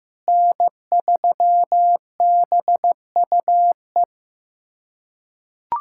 Morse code, heard as a keyed tone, about 15 wpm, 700 Hz. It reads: N3BUE
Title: USA Callsign DE Callsign - 5 Characters - 15wpm